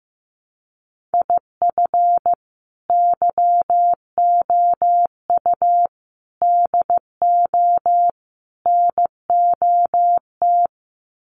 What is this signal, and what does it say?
Morse code, heard as a keyed tone, about 14 wpm, 700 Hz. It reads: IF YOU DO NOT
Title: Sets of 4 Words 15wpm